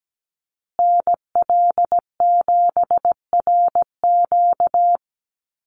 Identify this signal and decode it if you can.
Morse code, heard as a keyed tone, about 17 wpm, 700 Hz. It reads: NL7RQ
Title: USA Call Signs Repeated 3x 17wpm